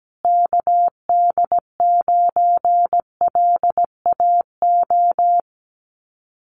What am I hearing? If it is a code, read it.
Morse code, heard as a keyed tone, about 17 wpm, 700 Hz. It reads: KD9LAO
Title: Common QSO Phrases Encore 17wpm